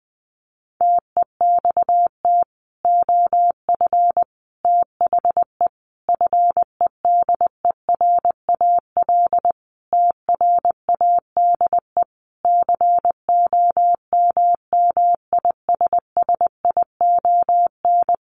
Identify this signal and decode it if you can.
Morse code, heard as a keyed tone, about 20 wpm, 700 Hz. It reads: TEXT OF THE FEDERAL TRADE COMMISSION
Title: Sentences from Top 1400 Words 20wpm